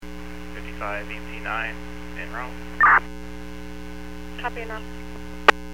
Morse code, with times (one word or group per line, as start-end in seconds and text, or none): none